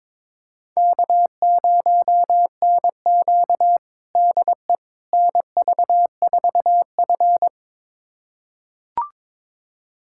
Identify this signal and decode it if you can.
Morse code, heard as a keyed tone, about 22 wpm, 700 Hz. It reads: K0NQ DE NV4F
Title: USA Callsign DE Callsign - 4 Characters - 22wpm